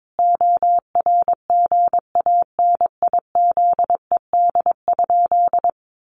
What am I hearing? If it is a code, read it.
Morse code, heard as a keyed tone, about 22 wpm, 700 Hz. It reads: ORGANIZED?
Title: Sentences from Top 1300 Words 22wpm